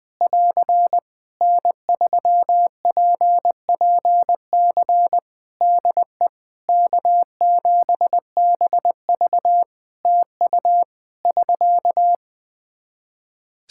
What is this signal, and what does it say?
Morse code, heard as a keyed tone, about 20 wpm, 700 Hz. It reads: <AR> N3PPC DE K7BV TU <SK>
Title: Practice QSO - Exchange 4 - Ending - 20wpm